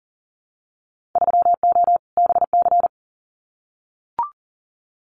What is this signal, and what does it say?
Morse code, heard as a keyed tone, about 40 wpm, 700 Hz. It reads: 3O BC
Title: Winter Field Day Exchanges Encore 40wpm